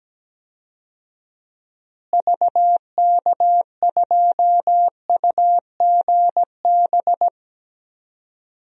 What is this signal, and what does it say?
Morse code, heard as a keyed tone, about 17 wpm, 700 Hz. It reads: VK2UGB
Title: DXCC-100 Call Signs - 6 Characters - Repeated 3x - 17wpm